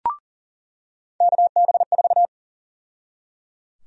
Morse code, heard as a keyed tone, about 37 wpm, 700 Hz. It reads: KB4
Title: USA Call Sign Prefixes - 40wpm